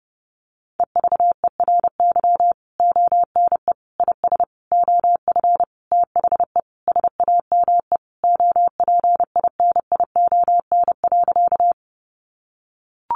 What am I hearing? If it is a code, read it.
Morse code, heard as a keyed tone, about 28 wpm, 700 Hz. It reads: EVERY ONE IS OF THE SAME OPINION.
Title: Sentences from Top 1200 Words 30wpm